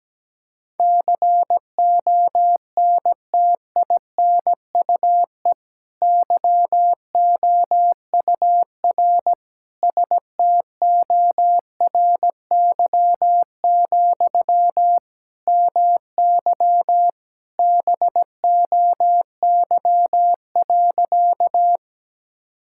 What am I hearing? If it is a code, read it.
Morse code, heard as a keyed tone, about 17 wpm, 700 Hz. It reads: CONTINUE YOUR STORY, MY BOY.
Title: Sentences from Top 1000 Words - Review Part 6 - 17wpm